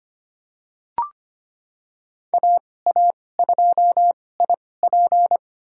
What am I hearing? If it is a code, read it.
Morse code, heard as a keyed tone, about 25 wpm, spaced as for 18 wpm, 700 Hz. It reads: AA2IP
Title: USA Call Signs - 5 Characters - 2x Character Spacing - 25wpm